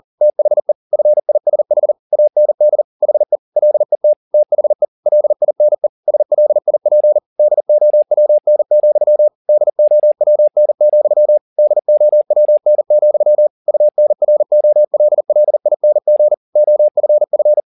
Morse code, heard as a keyed tone, about 40 wpm, 600 Hz. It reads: THE FISH AND HE LET THE LINE SLIP DOWN, DOWN, DOWN, UNROLLING OFF